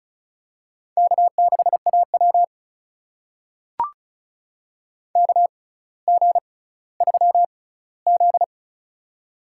Morse code, heard as a keyed tone, about 36 wpm, spaced as for 18 wpm, 700 Hz. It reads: K6AW KG3Z
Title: USA Call Signs - 6x Character Spacing - 35wpm